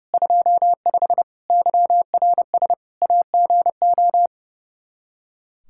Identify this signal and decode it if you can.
Morse code, heard as a keyed tone, about 30 wpm, 700 Hz. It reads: 25 YRS AGO
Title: K1USN SST QSO Practice 30wpm